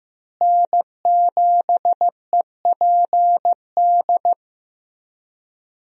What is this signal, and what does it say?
Morse code, heard as a keyed tone, about 15 wpm, 700 Hz. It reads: N7EPD
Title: USA Call Signs - 5 Characters - Repeated 2x - 15wpm